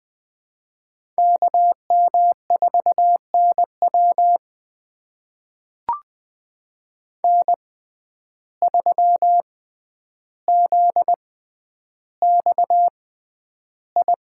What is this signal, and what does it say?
Morse code, heard as a keyed tone, about 20 wpm, spaced as for 10 wpm, 700 Hz. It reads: KM4NW N3ZXI
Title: USA Call Signs - 5 Characters - 6x Character Spacing - 20wpm